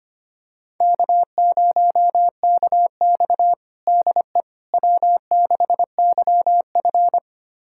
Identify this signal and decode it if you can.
Morse code, heard as a keyed tone, about 25 wpm, 700 Hz. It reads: K0KX DE W6YF
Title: USA Callsign DE Callsign - 4 Characters - 25wpm